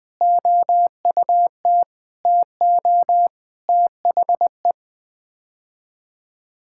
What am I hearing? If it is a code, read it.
Morse code, heard as a keyed tone, about 20 wpm, 700 Hz. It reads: OUT TO THE
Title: Sets of 8 Words - 20wpm